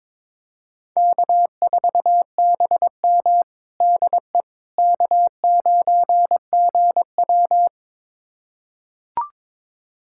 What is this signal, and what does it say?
Morse code, heard as a keyed tone, about 21 wpm, 700 Hz. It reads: K4BM DE K9GW
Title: USA Callsign DE Callsign - 4 Characters - 22wpm